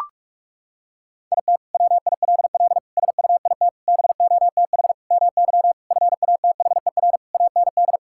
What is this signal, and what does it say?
Morse code, heard as a keyed tone, about 41 wpm, 700 Hz. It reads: IT WILL SUIT BOTH MY FATHER AND
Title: Sentences from Top 1600 Words 45wpm